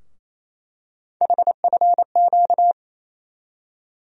Morse code, heard as a keyed tone, about 28 wpm, 700 Hz. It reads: HFQ
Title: USA Call Sign Postfixes - 3 Letters 28wpm